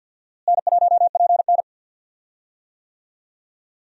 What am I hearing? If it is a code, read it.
Morse code, heard as a keyed tone, about 45 wpm, 700 Hz. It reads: N1PN
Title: USA Call Signs - 4 Characters - Repeated 2x - 50wpm